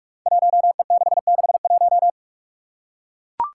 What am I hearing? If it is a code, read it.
Morse code, heard as a keyed tone, about 44 wpm, 700 Hz. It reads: 1E661
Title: Five Challenging Letter-Number Combinations 45wpm